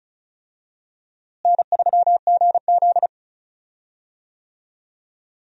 Morse code, heard as a keyed tone, about 34 wpm, 700 Hz. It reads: N3GZ
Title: USA Call Signs - 4 Characters - Repeated 2x - 35wpm